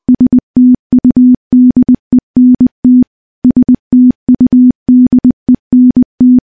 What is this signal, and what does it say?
Morse code, heard as a keyed tone, about 20 wpm, 250 Hz. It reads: STUDENT STUDENT
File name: morse-training-20WPM